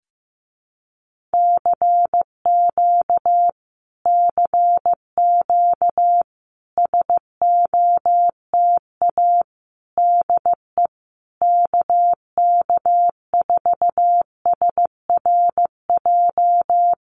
Morse code, heard as a keyed tone, about 15 wpm, 700 Hz. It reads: CQ CQ SOTA DE KK4SRJ
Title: Common QSO Phrases Encore 15wpm